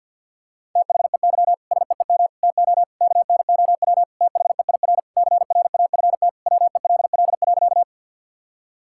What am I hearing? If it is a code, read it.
Morse code, heard as a keyed tone, about 50 wpm, 700 Hz. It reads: THEY SEEM TO KNOW THEIR CRAFT WELL.
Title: Sentences from Top 1000 Words - Review Part 6 - 50wpm